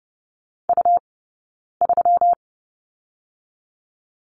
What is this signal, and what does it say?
Morse code, heard as a keyed tone, about 30 wpm, 700 Hz. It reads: U 3
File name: Lesson 20d - Number 3 - Two Characters 30wpm